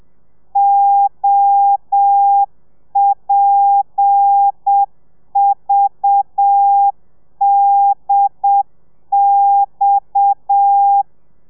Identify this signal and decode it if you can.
Morse code, heard as a keyed tone, about 7 wpm, 800 Hz. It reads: OPVDX
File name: telegrafia / i2tfj / Lez7 30